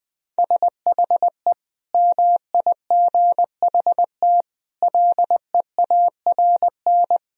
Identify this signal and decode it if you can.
Morse code, heard as a keyed tone, about 20 wpm, 700 Hz. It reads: SHE MIGHT LEARN
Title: Sentences from Top 1000 Words - Review Part 4 - 20wpm